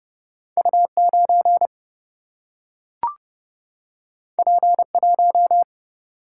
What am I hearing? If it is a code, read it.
Morse code, heard as a keyed tone, about 30 wpm, 700 Hz. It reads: U9 P1
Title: Single-Letter Single-Number 30wpm